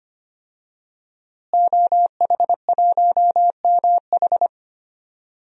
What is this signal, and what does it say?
Morse code, heard as a keyed tone, about 25 wpm, 700 Hz. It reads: OH1MH
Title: DXCC-100 Call Signs - 5 Characters - Repeated 2x - 25wpm